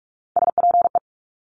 Morse code, heard as a keyed tone, about 46 wpm, 700 Hz. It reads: SPE
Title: Top Three Letter Combinations in Words 45wpm